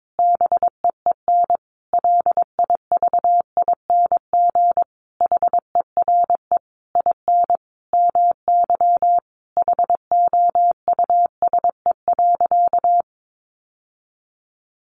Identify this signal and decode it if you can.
Morse code, heard as a keyed tone, about 22 wpm, 700 Hz. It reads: BEEN LIVING HERE IN MY HOUSE.
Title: Sentences from Top 1100 Words 22wpm